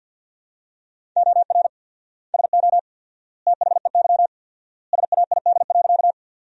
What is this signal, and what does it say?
Morse code, heard as a keyed tone, about 50 wpm, 700 Hz. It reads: OR SO THEY SAID.
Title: 4x Word-Spacing - Sentences from Top 100 Words - 50wpm